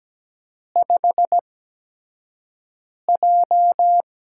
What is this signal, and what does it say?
Morse code, heard as a keyed tone, about 17 wpm, spaced as for 5 wpm, 700 Hz. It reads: 5J
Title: USA Call Signs - 4 Characters - 8x Character Spacing - 17wpm